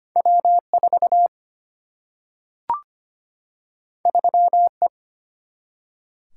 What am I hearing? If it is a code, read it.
Morse code, heard as a keyed tone, about 25 wpm, 700 Hz. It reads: W4 3E
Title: Single-Letter Single-Number 25wpm